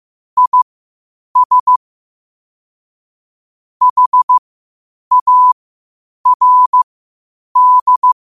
Morse code, heard as a keyed tone, about 15 wpm, spaced as for 8 wpm, 1000 Hz. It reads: IS HARD
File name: qotdfiles / 8wpm